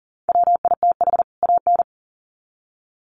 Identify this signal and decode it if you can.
Morse code, heard as a keyed tone, about 40 wpm, 700 Hz. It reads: WITH AN
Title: Sets of 2 Words 40wpm